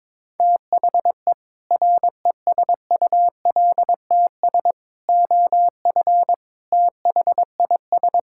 Morse code, heard as a keyed tone, about 22 wpm, 700 Hz. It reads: THE RESULTS OF THIS
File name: practice / 22wpm / Build A Phrase - 5-Word Phrases - 22wpm